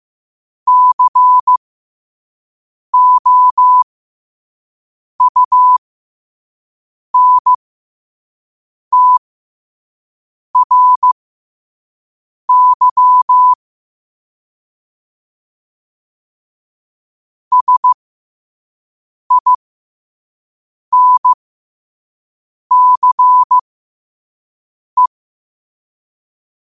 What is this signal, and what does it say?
Morse code, heard as a keyed tone, about 15 wpm, spaced as for 5 wpm, 1000 Hz. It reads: COUNTRY SINCE
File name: qotdfiles / 5wpm